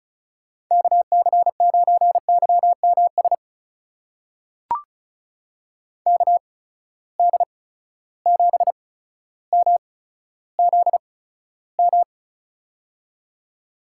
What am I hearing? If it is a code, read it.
Morse code, heard as a keyed tone, about 36 wpm, spaced as for 15 wpm, 700 Hz. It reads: KC9YMS KD7MZM
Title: USA Call Signs - 6 Characters - 8x Character Spacing - 35wpm